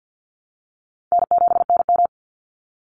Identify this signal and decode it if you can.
Morse code, heard as a keyed tone, about 50 wpm, 700 Hz. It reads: N7NM
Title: Call Signs - 50wpm